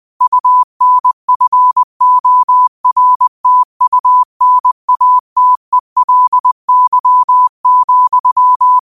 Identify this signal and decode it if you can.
Morse code, heard as a keyed tone, about 20 wpm, 1000 Hz. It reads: UNFORTUNATELY,
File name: qotdfiles / 20wpm